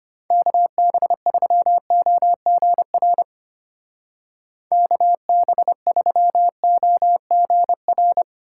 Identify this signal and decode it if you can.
Morse code, heard as a keyed tone, about 31 wpm, 700 Hz. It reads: KB3OGR KB3OGR
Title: Speed Racing - USA 2x3 Call Signs 25wpm